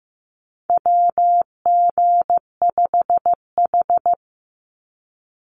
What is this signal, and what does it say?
Morse code, heard as a keyed tone, about 15 wpm, 700 Hz. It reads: WG5H
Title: Call Signs Repeated 3x - 15wpm